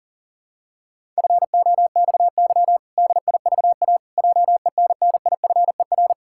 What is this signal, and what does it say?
Morse code, heard as a keyed tone, about 40 wpm, 700 Hz. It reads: FOXY DIVA JENNIFER
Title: Pangram Challenge v2 - 40wpm